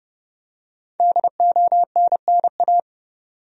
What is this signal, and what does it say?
Morse code, heard as a keyed tone, about 30 wpm, 700 Hz. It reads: DONNA
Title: USA Common Names - 30wpm